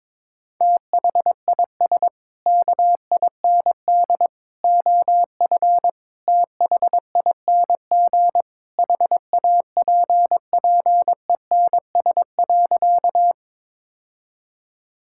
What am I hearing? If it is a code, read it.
Morse code, heard as a keyed tone, about 22 wpm, 700 Hz. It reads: THIS KIND OF THING HAPPENS.
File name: practice / 22wpm / Sentences with 3-Word Phrases - 22wpm